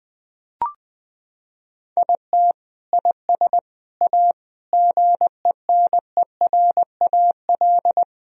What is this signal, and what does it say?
Morse code, heard as a keyed tone, about 21 wpm, 700 Hz. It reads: IT IS A GENERAL
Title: Sentences from Top 1000 Words - Review Part 3 - 20wpm